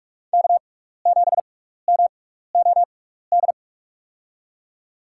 Morse code, heard as a keyed tone, about 42 wpm, spaced as for 15 wpm, 700 Hz. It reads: K7MOD
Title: USA Call Signs - 5 Characters - 6x Character Spacing - 45wpm